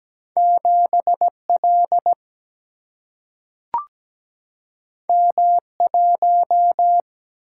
Morse code, heard as a keyed tone, about 17 wpm, 700 Hz. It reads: 7L M1
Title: Single-Letter Single-Number 17wpm